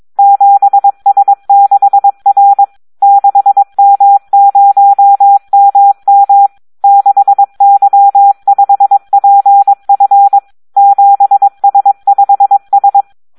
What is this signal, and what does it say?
Morse code, heard as a keyed tone, about 22 wpm, 800 Hz. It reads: 7S6R 6M0MM 6Y5PF 7S5S
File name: Audio practica CW / SD 1 22